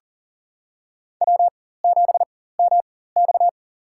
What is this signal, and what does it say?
Morse code, heard as a keyed tone, about 40 wpm, spaced as for 19 wpm, 700 Hz. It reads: W7MX
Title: USA Call Signs - 4 Characters - 4x Character Spacing - 40wpm